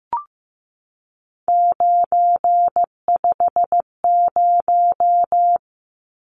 Morse code, heard as a keyed tone, about 15 wpm, 700 Hz. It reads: 950
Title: Numbers - 3-Digit 15wpm